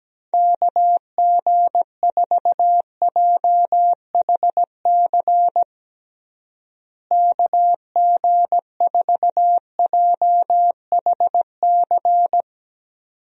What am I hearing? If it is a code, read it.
Morse code, heard as a keyed tone, about 17 wpm, 700 Hz. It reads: KG4JHC KG4JHC
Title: USA Call Signs - 6 Characters - Repeated 2x - 17wpm